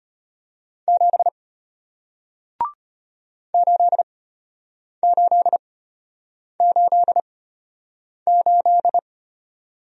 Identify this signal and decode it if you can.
Morse code, heard as a keyed tone, about 33 wpm, spaced as for 8 wpm, 700 Hz. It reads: 7 8888